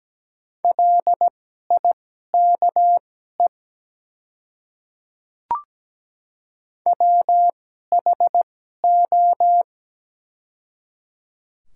Morse code, heard as a keyed tone, about 17 wpm, spaced as for 12 wpm, 690 Hz. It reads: LIKE WHO
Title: 2x Character Spacing - Top 100 Words - 17wpm